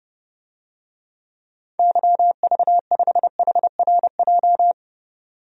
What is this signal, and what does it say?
Morse code, heard as a keyed tone, about 30 wpm, 700 Hz. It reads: YV5HRJ
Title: DXCC-100 Call Signs - 6 Characters - 30wpm